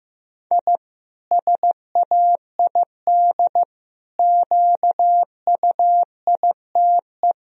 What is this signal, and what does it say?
Morse code, heard as a keyed tone, about 15 wpm, 700 Hz. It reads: I SAID QUITE
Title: Sentences from Top 1000 Words - Review Part 4 - 15wpm